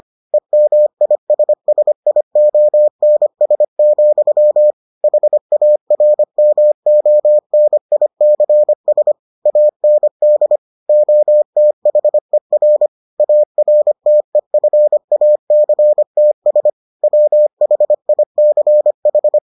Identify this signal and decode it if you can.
Morse code, heard as a keyed tone, about 25 wpm, 600 Hz. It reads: EMISSIONS, HARMONICS AND OTHER ARTEFACTS WHICH